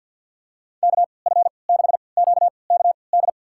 Morse code, heard as a keyed tone, about 50 wpm, spaced as for 28 wpm, 700 Hz. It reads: KF6QKD
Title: USA Call Signs - 6 Characters - 3x Character Spacing - 50wpm